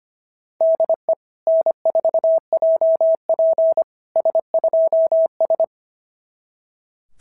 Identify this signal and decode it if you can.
Morse code, heard as a keyed tone, about 25 wpm, 660 Hz. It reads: DE N4JP S2S
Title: SOTA QSO Practice 25wpm